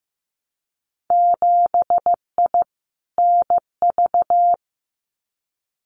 Morse code, heard as a keyed tone, about 15 wpm, 700 Hz. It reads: 7I NV
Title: Winter Field Day Exchanges Encore 15wpm